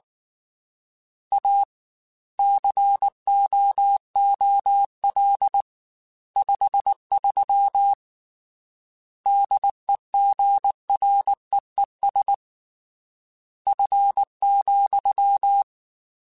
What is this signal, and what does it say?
Morse code, heard as a keyed tone, about 19 wpm, 800 Hz. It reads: A COOL 53 DEGREES F,
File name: Weather Sets / Weather Sets-mp3s / Weather 5 1 @18